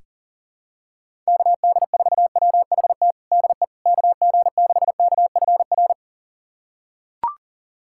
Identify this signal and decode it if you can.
Morse code, heard as a keyed tone, about 37 wpm, 700 Hz. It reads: KD4WHT DE KG6KFR
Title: USA Callsign DE Callsign - 6 Characters - 40wpm